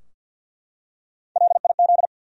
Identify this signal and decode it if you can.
Morse code, heard as a keyed tone, about 53 wpm, 700 Hz. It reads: LIZ
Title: USA Common Names - Max 3 Letters - 50wpm